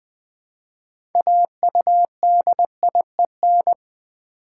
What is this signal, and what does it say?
Morse code, heard as a keyed tone, about 21 wpm, 700 Hz. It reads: AUDIEN
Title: Build A Word - 2 Letters At A Time - 8-Letter Words - 20wpm